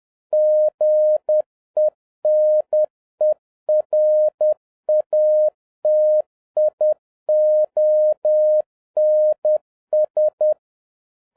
Morse code, heard as a keyed tone, about 10 wpm, 610 Hz. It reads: GENERATIONS